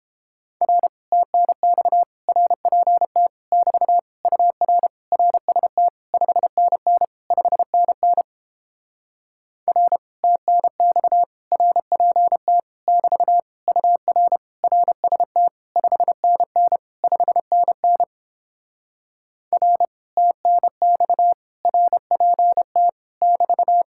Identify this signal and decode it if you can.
Morse code, heard as a keyed tone, about 29 wpm, 700 Hz. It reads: R TNX RPT <BT> UR RST 5NN 5NN R TNX RPT <BT> UR RST 5NN 5NN R TNX RPT <BT>